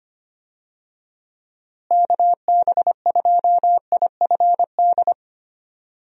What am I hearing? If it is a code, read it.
Morse code, heard as a keyed tone, about 25 wpm, 700 Hz. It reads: KB2IFD
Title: USA Call Signs - 6 Characters - Repeated 2x - 25wpm